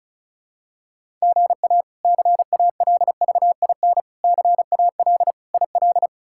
Morse code, heard as a keyed tone, about 34 wpm, 700 Hz. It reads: GA CALVIN CAL IL
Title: K1USN SST QSO Practice - 35wpm